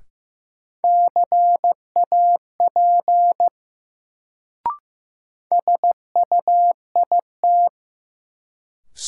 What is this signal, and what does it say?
Morse code, heard as a keyed tone, about 15 wpm, 700 Hz. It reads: CAP SUIT
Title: Sentences from Top 1600 Words - New Word Drill - 15wpm